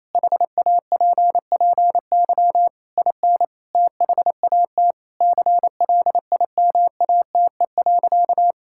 Morse code, heard as a keyed tone, about 28 wpm, 700 Hz. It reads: HAPPY IN THAT CLIMATE.
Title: Sentences from Top 1600 Words 28wpm